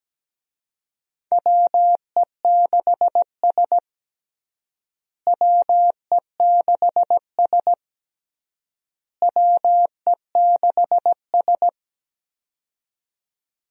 Morse code, heard as a keyed tone, about 17 wpm, 700 Hz. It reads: WE6S WE6S WE6S
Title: USA Call Signs - 4 Characters - Repeated 3x - 17wpm